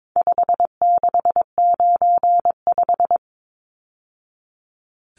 Morse code, heard as a keyed tone, about 22 wpm, 700 Hz. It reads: 5695
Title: Numbers - 4-Digit 22wpm